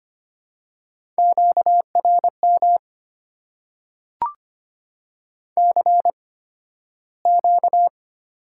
Morse code, heard as a keyed tone, about 25 wpm, spaced as for 13 wpm, 690 Hz. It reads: QRM CQ
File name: Top 90 QSO Elements - 8x Character Spacing - 25wpm